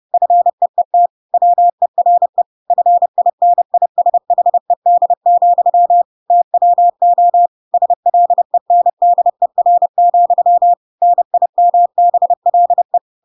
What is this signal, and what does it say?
Morse code, heard as a keyed tone, about 30 wpm, 700 Hz. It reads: FEET WERE FINISHED, TWO SLENDER, NIMBLE
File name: Pinocchio / Pinocchio ch8 30 30 ik1hge